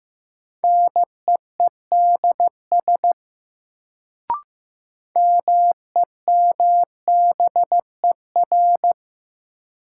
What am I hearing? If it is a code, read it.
Morse code, heard as a keyed tone, about 15 wpm, 700 Hz. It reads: NEEDS MEMBER
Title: Sets of 3 Words - New Word Drill - 15wpm